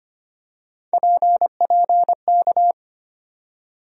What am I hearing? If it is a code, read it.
Morse code, heard as a keyed tone, about 25 wpm, 700 Hz. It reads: PPK
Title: USA Call Sign Postfixes - 3 Letters 25wpm